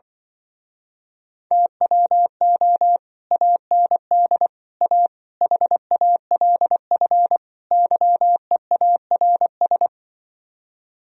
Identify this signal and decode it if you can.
Morse code, heard as a keyed tone, about 24 wpm, 700 Hz. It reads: TWO AND A HALF YEARS
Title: Speed-Racing - Sets of 5 Words 20wpm